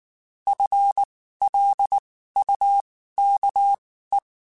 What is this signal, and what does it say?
Morse code, heard as a keyed tone, about 19 wpm, spaced as for 14 wpm, 800 Hz. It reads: FLUKE
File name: Spaced / FL Runs sp mp3s / FL Runs sp @18